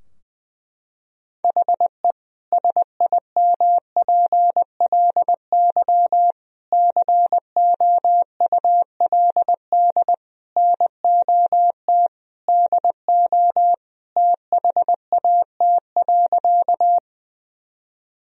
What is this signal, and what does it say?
Morse code, heard as a keyed tone, about 20 wpm, 700 Hz. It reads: HE SIMPLY COULD NOT DO THAT.
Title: Sentences from Top 1000 Words - Review Part 4 - 20wpm